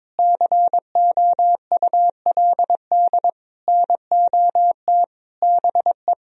Morse code, heard as a keyed tone, about 22 wpm, 700 Hz. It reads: COULD NOT BE
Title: Sentences from Top 1000 Words - Review Part 1 - 22wpm